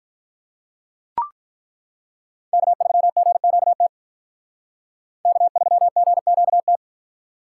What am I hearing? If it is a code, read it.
Morse code, heard as a keyed tone, about 43 wpm, 700 Hz. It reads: K3GQT K3GQT
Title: Speed Racing - USA 1x3 Call Signs 35wpm